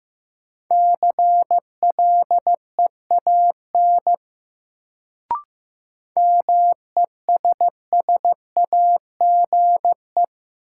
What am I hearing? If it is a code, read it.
Morse code, heard as a keyed tone, about 15 wpm, 700 Hz. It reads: CLEAN MESSAGE
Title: Sentences from Top 1100 Words - New Word Drill - 15wpm